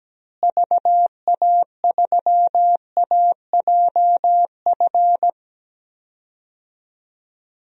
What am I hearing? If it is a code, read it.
Morse code, heard as a keyed tone, about 17 wpm, 700 Hz. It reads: VA3AJF
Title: DXCC-100 Call Signs - 6 Characters - Repeated 3x - 17wpm